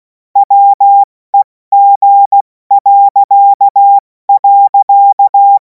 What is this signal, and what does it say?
Morse code, heard as a keyed tone, about 16 wpm, spaced as for 14 wpm, 800 Hz. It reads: WEG..
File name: Morsekurs / Witze / Witz 17. März 2026 70 F80